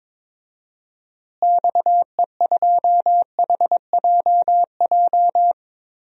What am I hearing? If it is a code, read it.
Morse code, heard as a keyed tone, about 22 wpm, 700 Hz. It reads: XE2HJJ
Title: DXCC-100 Call Signs - 6 Characters - Repeated 3x - 22wpm